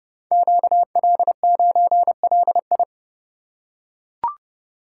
Morse code, heard as a keyed tone, about 30 wpm, 700 Hz. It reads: QL9LI
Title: Code Groups - 5 Letter:Numbers - 30wpm